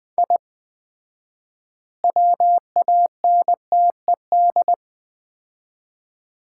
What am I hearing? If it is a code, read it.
Morse code, heard as a keyed tone, about 20 wpm, 700 Hz. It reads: I WANTED
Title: 4x Word-Spacing - Sets of 3 Words - 20wpm